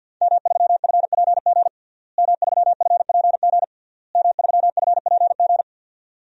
Morse code, heard as a keyed tone, about 52 wpm, 700 Hz. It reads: M3FPG M3FPG M3FPG
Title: DXCC-100 Call Signs - 5 Characters - Repeated 3x - 50wpm